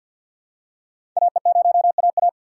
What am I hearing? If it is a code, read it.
Morse code, heard as a keyed tone, about 47 wpm, 700 Hz. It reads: AE0AA
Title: USA Call Signs - 5 Characters - 6x Character Spacing - 50wpm